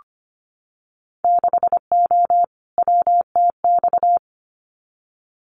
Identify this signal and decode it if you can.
Morse code, heard as a keyed tone, about 25 wpm, 700 Hz. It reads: 6O WTX
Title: Winter Field Day Exchanges Encore 25wpm